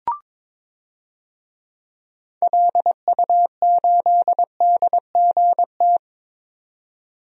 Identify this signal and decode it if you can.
Morse code, heard as a keyed tone, about 22 wpm, 700 Hz. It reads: LU8DGT
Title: DXCC-100 Call Signs - 6 Characters - 22wpm